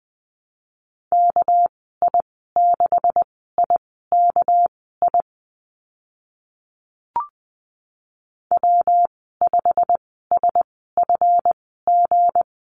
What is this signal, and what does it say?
Morse code, heard as a keyed tone, about 20 wpm, spaced as for 14 wpm, 700 Hz. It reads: KI6IKI W5SFG
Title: 200 Lesson 32g - Letter k - Call Signs 20wpm